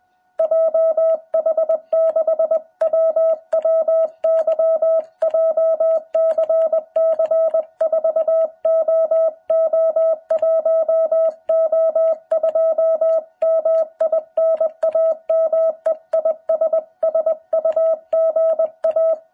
Morse code, heard as a keyed tone, about 20 wpm, 660 Hz. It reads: JH6WWY JCC4OO1O2 MI NAME IS SUGA
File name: jh8cbh / 2021cw